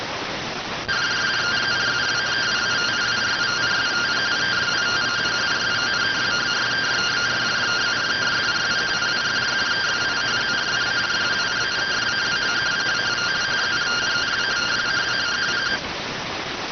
Two-tone teleprinter signal, about 170 Hz shift.